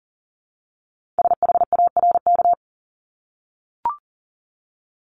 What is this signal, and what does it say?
Morse code, heard as a keyed tone, about 42 wpm, 700 Hz. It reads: SHARK
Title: Multi-letter Phonemes 40wpm